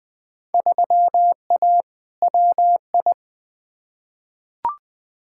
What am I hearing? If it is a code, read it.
Morse code, heard as a keyed tone, about 21 wpm, 700 Hz. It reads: 3A WI
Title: Field Day Exchanges - 20wpm